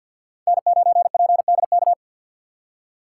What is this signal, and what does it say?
Morse code, heard as a keyed tone, about 53 wpm, 700 Hz. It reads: N9PDK
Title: USA Call Signs - 5 Characters - 2x Character Spacing - 50wpm